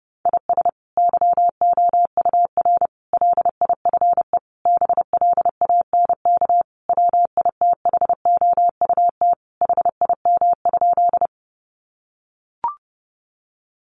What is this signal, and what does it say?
Morse code, heard as a keyed tone, about 30 wpm, 700 Hz. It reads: IS YOUR LIFE BLANK WITHOUT HIM?
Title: Sentences from Top 1500 Words 30wpm